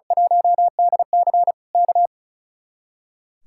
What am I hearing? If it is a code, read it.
Morse code, heard as a keyed tone, about 35 wpm, 700 Hz. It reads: 1DC K
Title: Speed-Racing Common QSO Phrases - 35wpm